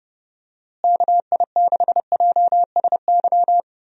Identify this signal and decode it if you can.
Morse code, heard as a keyed tone, about 30 wpm, 700 Hz. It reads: KI6JSY
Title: USA Call Signs - 6 Characters - Repeated 2x - 30wpm